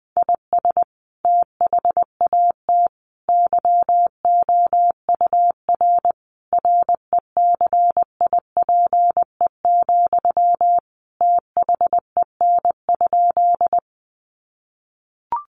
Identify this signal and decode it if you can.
Morse code, heard as a keyed tone, about 20 wpm, 700 Hz. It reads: IS THAT YOUR RECIPE, THEN?
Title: Sentences from Top 1200 Words 20wpm